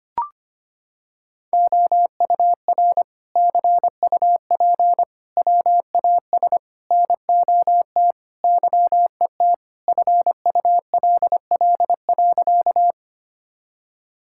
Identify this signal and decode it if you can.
Morse code, heard as a keyed tone, about 25 wpm, 700 Hz. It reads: OUR CUP WAS NOT YET FULL.
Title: Sentences from Top 1600 Words 25wpm